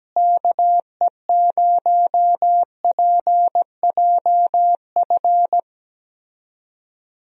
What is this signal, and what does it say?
Morse code, heard as a keyed tone, about 17 wpm, 700 Hz. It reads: KE0PJF
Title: USA Call Signs - 6 Characters - Repeated 3x - 17wpm